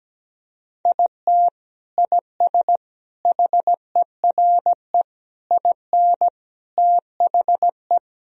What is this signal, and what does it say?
Morse code, heard as a keyed tone, about 17 wpm, 700 Hz. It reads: IT IS HERE IN THE
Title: Sentences from Top 1000 Words - Review Part 3 - 17wpm